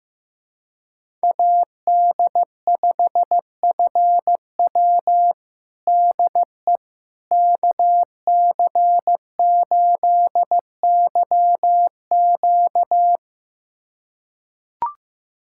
Morse code, heard as a keyed tone, about 15 wpm, 700 Hz. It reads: AD5FW DE KC8YQ
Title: USA Callsign DE Callsign - 5 Characters - 15wpm